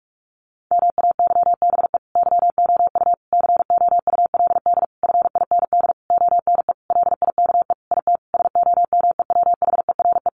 Morse code, heard as a keyed tone, about 44 wpm, 700 Hz. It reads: MAYBE YOU COULD FIND ONE LIKE IT SOMEWHERE
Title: Sentences from Top 1400 Words 45wpm